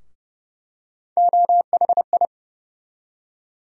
Morse code, heard as a keyed tone, about 30 wpm, 700 Hz. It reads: OHI
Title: USA Call Sign Postfixes - 3 Letters 30wpm